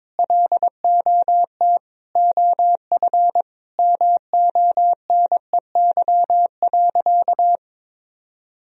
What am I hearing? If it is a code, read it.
Morse code, heard as a keyed tone, about 22 wpm, 700 Hz. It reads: LOT OF MONEY.
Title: Sentences from Top 1000 Words - Review Part 4 - 22wpm